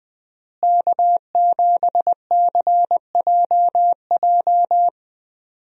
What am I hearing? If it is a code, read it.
Morse code, heard as a keyed tone, about 20 wpm, 700 Hz. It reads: K7CJJ
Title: USA Call Signs - 5 Characters - Repeated 3x - 20wpm